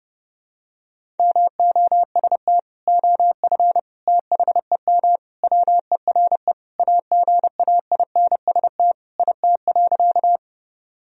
Morse code, heard as a keyed tone, about 30 wpm, 700 Hz. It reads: MOST OF THEM WERE AGAINST IT.
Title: Sentences from Top 1000 Words - Review Part 1 - 30wpm